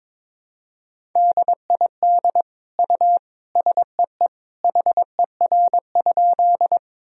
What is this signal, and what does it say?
Morse code, heard as a keyed tone, about 22 wpm, 700 Hz. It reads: DID U SEE HER?
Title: Sentences from Top 100 Words and Abbreviations 22wpm